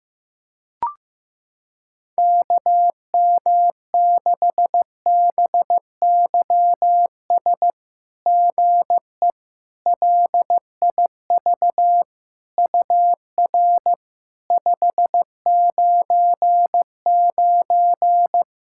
Morse code, heard as a keyed tone, about 15 wpm, 690 Hz. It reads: KM6BYS GE LIV UR 599
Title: SOTA QSO Practice 15wpm